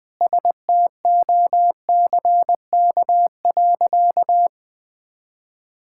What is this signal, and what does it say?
Morse code, heard as a keyed tone, about 20 wpm, 700 Hz. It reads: STOCK.
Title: Sentences from Top 1000 Words - Review Part 5 - 20wpm